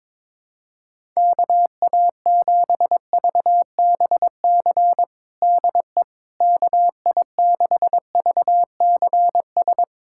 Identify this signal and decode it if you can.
Morse code, heard as a keyed tone, about 22 wpm, 700 Hz. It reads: KA7VBC DE KI6VCS
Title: USA Callsign DE Callsign - 6 Characters - 22wpm